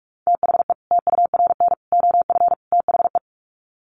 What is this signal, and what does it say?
Morse code, heard as a keyed tone, about 44 wpm, 700 Hz. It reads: THE TURN OF THE
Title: Sets of 4 Words 45wpm